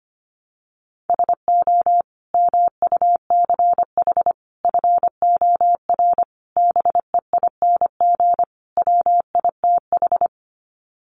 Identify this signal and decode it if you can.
Morse code, heard as a keyed tone, about 25 wpm, 700 Hz. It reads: SO MUCH FOR BEING WITH
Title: Sets of 5 Words Encore 25wpm